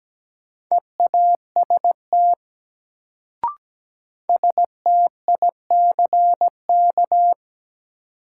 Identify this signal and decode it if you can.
Morse code, heard as a keyed tone, about 17 wpm, 700 Hz. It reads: EAST STICK
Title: Sentences from Top 1200 Words - New Word Drill Alt - 17wpm